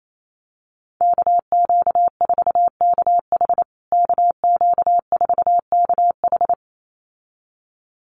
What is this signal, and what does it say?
Morse code, heard as a keyed tone, about 28 wpm, 700 Hz. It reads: KQ4KH KQ4KH
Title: USA Call Signs Repeated 2x 28wpm